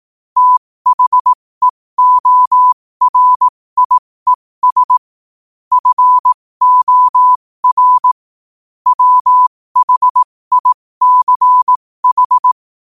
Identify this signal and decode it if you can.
Morse code, heard as a keyed tone, about 18 wpm, spaced as for 15 wpm, 1000 Hz. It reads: THEORIES FOR WHICH
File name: qotdfiles / 15wpm